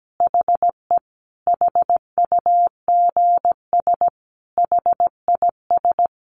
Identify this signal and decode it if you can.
Morse code, heard as a keyed tone, about 17 wpm, 700 Hz. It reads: HE HUGS HIS
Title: Easy Phrases 17wpm